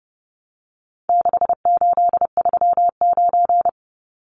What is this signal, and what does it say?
Morse code, heard as a keyed tone, about 30 wpm, 700 Hz. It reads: 6839
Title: Numbers - 4-Digit 30wpm